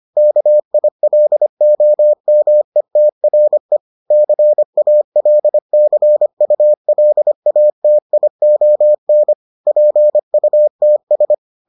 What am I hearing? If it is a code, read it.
Morse code, heard as a keyed tone, about 25 wpm, 600 Hz. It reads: KILOMETRE CALCULATION PUTS